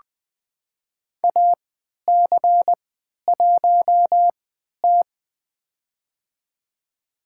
Morse code, heard as a keyed tone, about 21 wpm, spaced as for 11 wpm, 700 Hz. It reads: AC1T
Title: USA Call Signs - 4 Characters - 3x Character Spacing - 20wpm